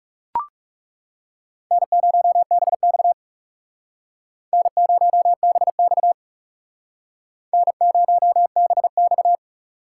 Morse code, heard as a keyed tone, about 40 wpm, 700 Hz. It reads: N0BX N0BX N0BX